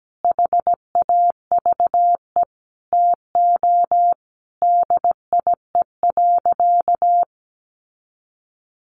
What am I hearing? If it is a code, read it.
Morse code, heard as a keyed tone, about 17 wpm, 700 Hz. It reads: HAVE TO DIE.
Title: Sentences from Top 1300 Words 17wpm